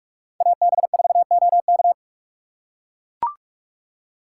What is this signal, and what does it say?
Morse code, heard as a keyed tone, about 43 wpm, 700 Hz. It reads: AB4OK